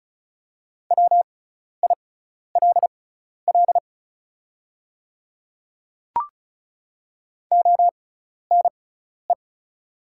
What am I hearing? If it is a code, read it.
Morse code, heard as a keyed tone, about 34 wpm, spaced as for 12 wpm, 700 Hz. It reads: WILL ONE
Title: 6x Character Spacing - Top 100 Words - 35wpm